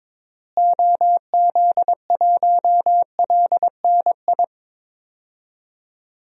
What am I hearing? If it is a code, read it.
Morse code, heard as a keyed tone, about 22 wpm, 700 Hz. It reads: OZ1LNI
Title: DXCC-100 Call Signs - 6 Characters - Repeated 3x - 22wpm